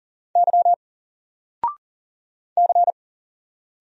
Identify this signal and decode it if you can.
Morse code, heard as a keyed tone, about 40 wpm, 700 Hz. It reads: Y C
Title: Single Character (Letter:Number:Punctuation:Prosign) 40wpm